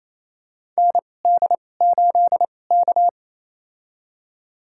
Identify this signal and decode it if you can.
Morse code, heard as a keyed tone, about 28 wpm, spaced as for 20 wpm, 700 Hz. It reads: ND8K
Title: USA Call Signs - 4 Characters - 2x Character Spacing - 28wpm